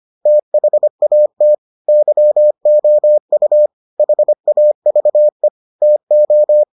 Morse code, heard as a keyed tone, about 25 wpm, 600 Hz. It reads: THAT YOU HAVE TO